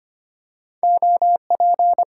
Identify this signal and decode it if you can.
Morse code, heard as a keyed tone, about 25 wpm, 700 Hz. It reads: OP
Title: Top 90 QSO Elements 25wpm